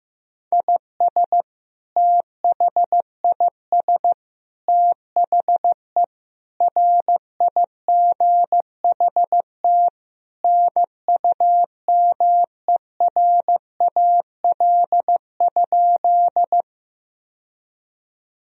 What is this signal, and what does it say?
Morse code, heard as a keyed tone, about 15 wpm, 700 Hz. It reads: IS THIS THE RIGHT NUMERAL?
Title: Sentences from Top 1000 Words - Review Part 2 - 15wpm